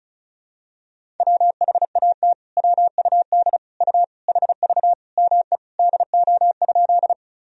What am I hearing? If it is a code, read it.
Morse code, heard as a keyed tone, about 36 wpm, 700 Hz. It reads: WHAT WUD U HV ME DO?
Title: Sentences from Top 200 Words and Abbreviations 35wpm